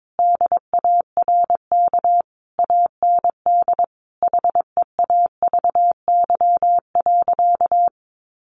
Morse code, heard as a keyed tone, about 22 wpm, 700 Hz. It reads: DARK AND HEAVY.
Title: Sentences from Top 1400 Words 22wpm